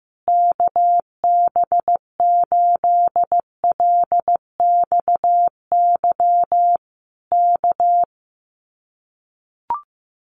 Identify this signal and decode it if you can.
Morse code, heard as a keyed tone, about 15 wpm, 700 Hz. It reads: KB8LXY K
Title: Common QSO Phrases Encore 15wpm